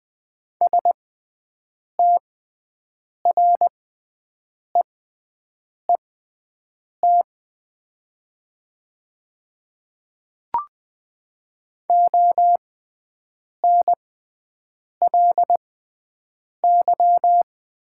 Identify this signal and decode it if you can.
Morse code, heard as a keyed tone, about 20 wpm, spaced as for 7 wpm, 700 Hz. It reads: STREET ONLY
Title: 6x Character Spacing - Top 500 Words - 20wpm